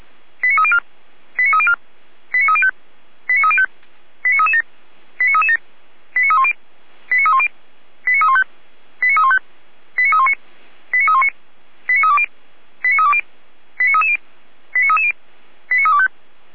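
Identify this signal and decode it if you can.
Five-tone selective call (ZVEI), heard as quick runs of five tones, tones of about 70 ms. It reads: 89304 89304 89305 89305 89337 89337 89310 89310 89315 89315 89319 89319 89320 89320 89330 89330 89325
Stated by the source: ZVEI five-tone selective call received with an SDR.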